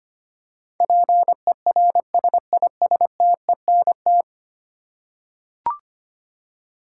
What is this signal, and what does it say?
Morse code, heard as a keyed tone, about 25 wpm, 700 Hz. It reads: PERSISTENT
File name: 10 Letter Words - First 3 Letters Given - 25wpm